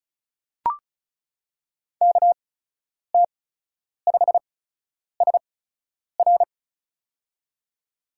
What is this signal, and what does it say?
Morse code, heard as a keyed tone, about 34 wpm, spaced as for 10 wpm, 700 Hz. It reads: KT5SR
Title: USA Call Signs - 5 Characters - 8x Character Spacing - 35wpm